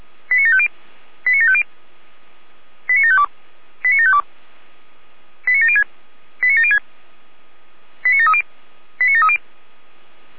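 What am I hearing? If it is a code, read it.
Five-tone selective call (ZVEI), heard as quick runs of five tones, tones of about 80 ms. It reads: 89750 89750 89752 89752 89706 89706 89730 89730
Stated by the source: Five-tone ZVEI selective call 89531 received on an SDR.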